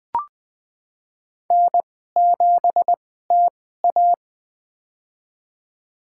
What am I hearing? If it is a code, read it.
Morse code, heard as a keyed tone, about 20 wpm, spaced as for 15 wpm, 700 Hz. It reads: N7TA
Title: USA Call Signs - 4 Characters - 2x Character Spacing - 20wpm